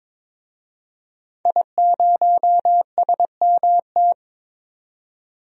Morse code, heard as a keyed tone, about 22 wpm, 700 Hz. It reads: I0SMT
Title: DXCC-100 Call Signs - 5 Characters - 22wpm